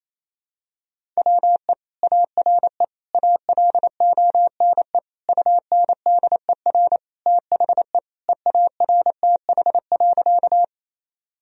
Morse code, heard as a keyed tone, about 28 wpm, 700 Hz. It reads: WE ARE ALONE UNDER THE EARTH.
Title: Sentences from Top 1000 Words - Review Part 1 - 28wpm